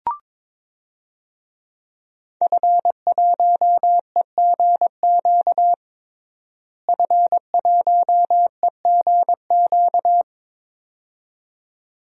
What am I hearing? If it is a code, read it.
Morse code, heard as a keyed tone, about 22 wpm, 700 Hz. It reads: F1EGQ F1EGQ
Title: DXCC-100 Call Signs - 5 Characters - Repeated 2x - 22wpm